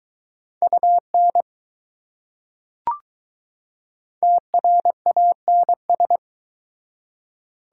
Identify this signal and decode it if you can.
Morse code, heard as a keyed tone, about 21 wpm, 700 Hz. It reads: UN TRANS